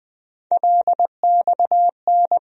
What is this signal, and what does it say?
Morse code, heard as a keyed tone, about 20 wpm, 700 Hz. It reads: LXN
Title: USA Call Sign Postfixes - 3 Letters 20wpm